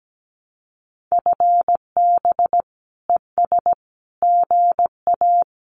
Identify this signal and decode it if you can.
Morse code, heard as a keyed tone, about 17 wpm, 700 Hz. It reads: FB ES GA
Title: Common QSO Phrases Encore 17wpm